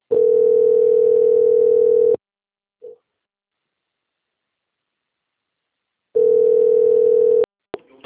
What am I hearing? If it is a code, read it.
Ringback tone (North America), 440 + 480 Hz.